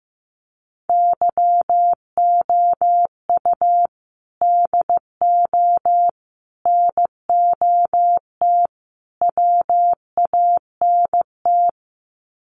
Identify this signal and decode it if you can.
Morse code, heard as a keyed tone, about 15 wpm, 700 Hz. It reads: YOU DO NOT WANT
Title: Sets of 4 Words 15wpm